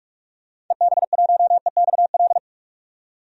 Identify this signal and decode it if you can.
Morse code, heard as a keyed tone, about 46 wpm, 700 Hz. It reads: EB1EXL
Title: DXCC-100 Call Signs - 6 Characters - 45wpm